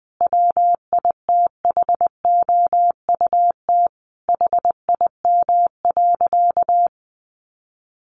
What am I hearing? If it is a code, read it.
Morse code, heard as a keyed tone, about 20 wpm, 700 Hz. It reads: WITHOUT HIM.
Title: Sentences from Top 1400 Words 20wpm